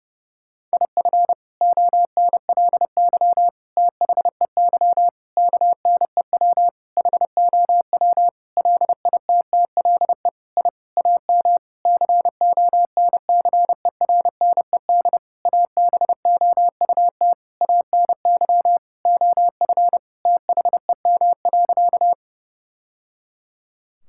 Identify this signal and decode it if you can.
Morse code, heard as a keyed tone, about 32 wpm, 700 Hz. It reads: IF ONLY THEY KNEW HOW LITTLE I AM CONCERNED ABOUT ANY OF THEM.
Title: Sentences from Top 1000 Words - Review Part 6 - 30wpm